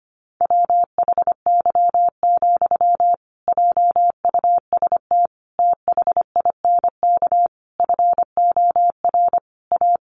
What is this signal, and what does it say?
Morse code, heard as a keyed tone, about 25 wpm, 700 Hz. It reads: WHY, JUST THINK FOR A
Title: Sentences from Top 1000 Words 25wpm